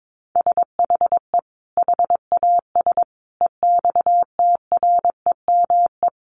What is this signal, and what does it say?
Morse code, heard as a keyed tone, about 22 wpm, 700 Hz. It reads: SHE HAS EXTREME
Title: Sentences from Top 1300 Words 22wpm